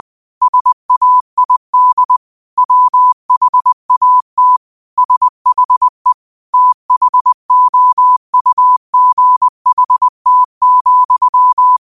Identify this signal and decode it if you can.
Morse code, heard as a keyed tone, about 20 wpm, 1000 Hz. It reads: SAID WHAT SHE THOUGHT,